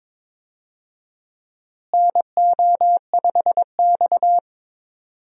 Morse code, heard as a keyed tone, about 22 wpm, 700 Hz. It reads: NO5X
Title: USA Call Signs - 4 Characters - Repeated 2x - 22wpm